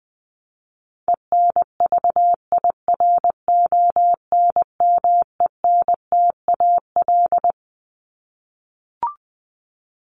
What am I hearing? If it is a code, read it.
Morse code, heard as a keyed tone, about 19 wpm, 700 Hz. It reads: ENVIRONMENTAL
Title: Top 1001-1250 Words 20wpm